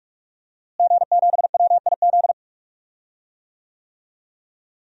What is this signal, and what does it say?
Morse code, heard as a keyed tone, about 46 wpm, 700 Hz. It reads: G7WIZ
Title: DXCC-100 Call Signs - 5 Characters - Repeated 3x - 45wpm